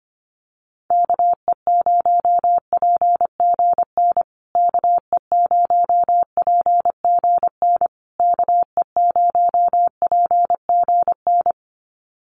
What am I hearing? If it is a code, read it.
Morse code, heard as a keyed tone, about 25 wpm, 700 Hz. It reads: KE0PGN KE0PGN KE0PGN
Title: USA Call Signs Repeated 3x 25wpm